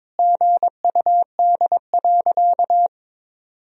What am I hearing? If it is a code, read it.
Morse code, heard as a keyed tone, about 22 wpm, 700 Hz. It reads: GUD.
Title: Sentences from Top 200 Words and Abbreviations 22wpm